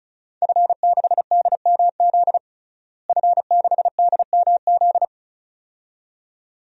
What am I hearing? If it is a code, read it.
Morse code, heard as a keyed tone, about 34 wpm, 700 Hz. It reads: F6DMZ F6DMZ
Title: DXCC-100 Call Signs - 5 Characters - Repeated 2x - 35wpm